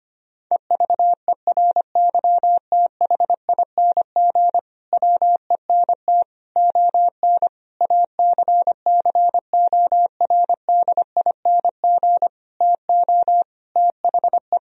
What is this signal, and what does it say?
Morse code, heard as a keyed tone, about 25 wpm, 700 Hz. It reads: EVERYTHING WENT ON ACCORDING TO THE
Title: Sentences from Top 1000 Words - Review Part 4 - 25wpm